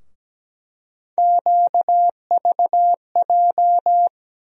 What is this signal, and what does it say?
Morse code, heard as a keyed tone, about 17 wpm, 700 Hz. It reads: QVJ
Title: USA Call Sign Postfixes - 3 Letters 17wpm